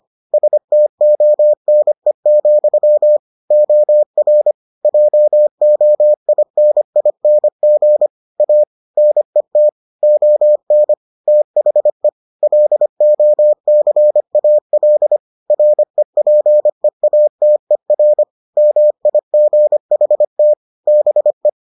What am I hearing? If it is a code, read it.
Morse code, heard as a keyed tone, about 25 wpm, 600 Hz. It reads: STONE, OR JOINING A NET ON THE LOCAL REPEATER MIGHT BE